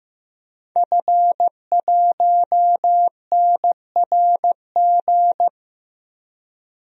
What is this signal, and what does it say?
Morse code, heard as a keyed tone, about 14 wpm, 700 Hz. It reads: F1NRG
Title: DXCC-100 Call Signs - 5 Characters - Repeated 3x - 15wpm